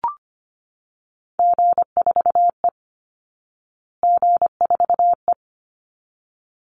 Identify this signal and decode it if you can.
Morse code, heard as a keyed tone, about 25 wpm, 700 Hz. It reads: G4E G4E
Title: Call Signs Repeated 3x - 25wpm